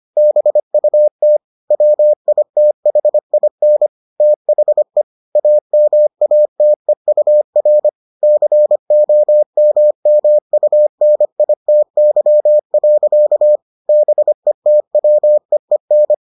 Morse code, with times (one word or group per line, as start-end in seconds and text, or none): 0.2-1.4 BUT
1.7-3.9 WITHIN
4.2-5.0 THE
5.4-7.9 AMATEUR
8.2-13.6 COMMUNITY.
13.9-16.1 BETWEEN